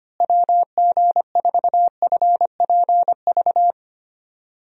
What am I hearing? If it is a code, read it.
Morse code, heard as a keyed tone, about 25 wpm, 700 Hz. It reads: WG4FPV
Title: USA Call Signs - 6 Characters - Repeated 2x - 25wpm